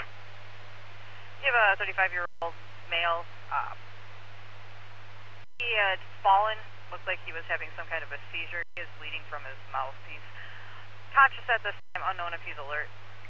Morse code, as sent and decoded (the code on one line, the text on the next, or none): none